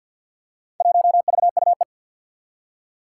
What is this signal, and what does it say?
Morse code, heard as a keyed tone, about 50 wpm, 700 Hz. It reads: 1VUE
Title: Four Challenging Letter-Number Combinations 50wpm